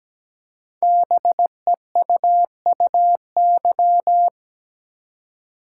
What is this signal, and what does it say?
Morse code, heard as a keyed tone, about 17 wpm, 700 Hz. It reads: BEUUY
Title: Code Groups - 5 Letters - 17wpm